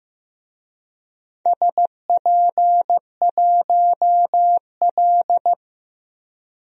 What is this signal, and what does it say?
Morse code, heard as a keyed tone, about 16 wpm, 700 Hz. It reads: SP1L
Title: DXCC-100 Call Signs - 4 Characters - 15wpm